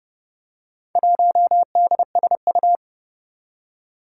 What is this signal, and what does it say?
Morse code, heard as a keyed tone, about 30 wpm, 700 Hz. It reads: 1DSU
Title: Four Challenging Letter-Number Combinations 30wpm